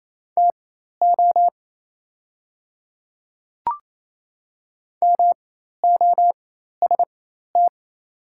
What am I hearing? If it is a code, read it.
Morse code, heard as a keyed tone, about 28 wpm, spaced as for 13 wpm, 700 Hz. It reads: TO MOST
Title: 4x Character Spacing - Top 100 Words - 28wpm